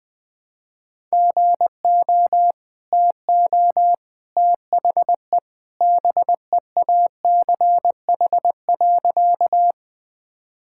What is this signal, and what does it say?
Morse code, heard as a keyed tone, about 20 wpm, 700 Hz. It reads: GO TO THE BEACH.
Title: Easy Sentences with Hints - 20wpm